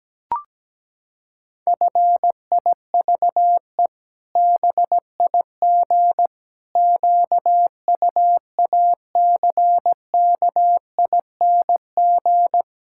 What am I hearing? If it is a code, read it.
Morse code, heard as a keyed tone, about 17 wpm, 700 Hz. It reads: FIVE BIG QUACKING
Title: Pangram Challenge v2 - 17wpm